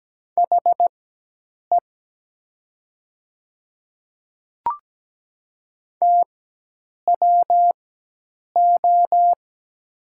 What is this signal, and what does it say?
Morse code, heard as a keyed tone, about 17 wpm, spaced as for 8 wpm, 700 Hz. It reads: HE TWO
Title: 4x Character Spacing - Top 300 Words - 17wpm